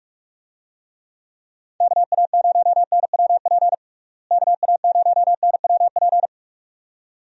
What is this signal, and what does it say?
Morse code, heard as a keyed tone, about 45 wpm, 700 Hz. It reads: KA0NWP KA0NWP
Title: USA Call Signs - 6 Characters - Repeated 2x - 45wpm